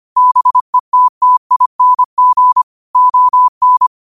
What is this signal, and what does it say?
Morse code, heard as a keyed tone, about 25 wpm, 1000 Hz. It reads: BETTING ON
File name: qotdfiles / 25wpm